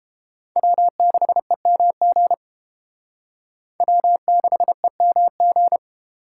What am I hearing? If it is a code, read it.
Morse code, heard as a keyed tone, about 28 wpm, 700 Hz. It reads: W6EMG W6EMG